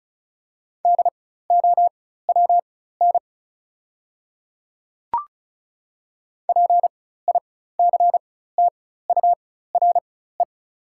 Morse code, heard as a keyed tone, about 34 wpm, spaced as for 16 wpm, 690 Hz. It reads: DOWN PICTURE
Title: 4x Character Spacing - Top 300 Words - 35wpm